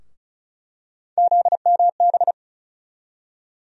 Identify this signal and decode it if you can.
Morse code, heard as a keyed tone, about 36 wpm, 700 Hz. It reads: ZMB
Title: USA Call Sign Postfixes - 3 Letters 35wpm